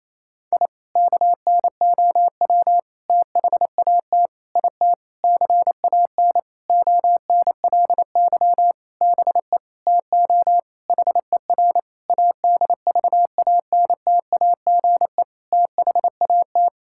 Morse code, heard as a keyed tone, about 28 wpm, 700 Hz. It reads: I KNOW THAT IT CAN ONLY BE TO HER ADVANTAGE THAT
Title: Sentences from Top 1000 Words - Review Part 6 - 28wpm